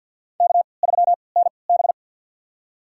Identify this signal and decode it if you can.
Morse code, heard as a keyed tone, about 51 wpm, spaced as for 28 wpm, 700 Hz. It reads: K3NB
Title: USA Call Signs - 4 Characters - 3x Character Spacing - 50wpm